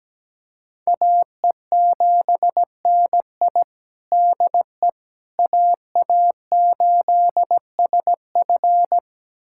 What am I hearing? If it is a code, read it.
Morse code, heard as a keyed tone, about 17 wpm, 700 Hz. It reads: AE7NI DE AA8SF
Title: USA Callsign DE Callsign - 5 Characters - 17wpm